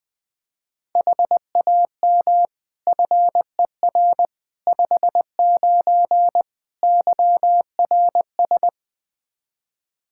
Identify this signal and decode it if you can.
Morse code, heard as a keyed tone, about 21 wpm, 700 Hz. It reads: HAM FER 59 YRS
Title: Practice QSO - Exchange 3 - More Info - 20wpm